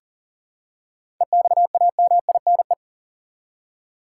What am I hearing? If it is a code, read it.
Morse code, heard as a keyed tone, about 40 wpm, 700 Hz. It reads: EXAMINE
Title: Seven Letter Words - 40wpm